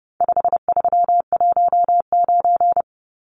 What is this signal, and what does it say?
Morse code, heard as a keyed tone, about 30 wpm, 700 Hz. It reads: 5319
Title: Numbers - 4-Digit 30wpm